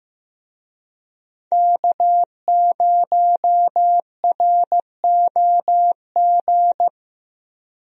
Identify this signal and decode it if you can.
Morse code, heard as a keyed tone, about 15 wpm, 700 Hz. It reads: K0ROG
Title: USA Call Signs - 5 Characters - Repeated 3x - 15wpm